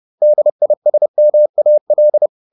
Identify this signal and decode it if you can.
Morse code, heard as a keyed tone, about 30 wpm, 600 Hz. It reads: DISMAL